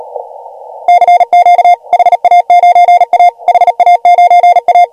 Morse code, heard as a keyed tone, about 39 wpm, 690 Hz. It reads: CQ HA9A HA9A
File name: AUDIO2019 / HG1G CW